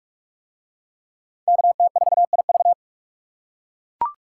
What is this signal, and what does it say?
Morse code, heard as a keyed tone, about 44 wpm, 700 Hz. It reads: KT4IV